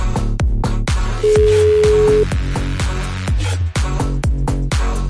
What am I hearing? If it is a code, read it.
Ringback tone (Europe, 425 Hz), with noise.